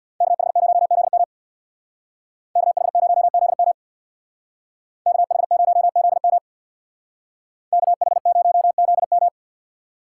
Morse code, heard as a keyed tone, about 64 wpm, 700 Hz. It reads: KH0ZM KH0ZM KH0ZM KH0ZM